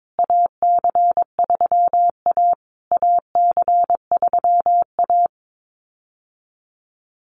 Morse code, heard as a keyed tone, about 22 wpm, 700 Hz. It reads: AC3A AC3A
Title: USA Call Signs Repeated 3x 22wpm